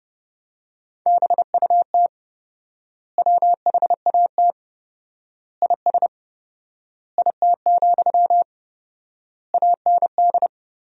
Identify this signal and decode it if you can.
Morse code, heard as a keyed tone, about 30 wpm, 700 Hz. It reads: BUT WHAT IS IT, AND
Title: 4x Word-Spacing - Sentences from Top 300 Words - 30wpm